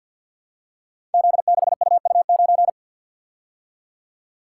Morse code, heard as a keyed tone, about 52 wpm, 700 Hz. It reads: Z6UU9
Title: Code Groups - 5 Letter:Numbers - 50wpm